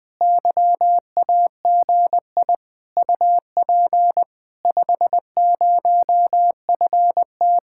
Morse code, heard as a keyed tone, about 20 wpm, 700 Hz. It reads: YAGI UP 50FT
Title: Speed-Racing Common QSO Phrases - 17wpm